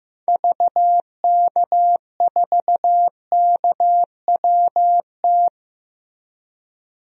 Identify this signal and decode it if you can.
Morse code, heard as a keyed tone, about 15 wpm, 700 Hz. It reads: VK4KWT
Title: DXCC-100 Call Signs - 6 Characters - Repeated 2x - 15wpm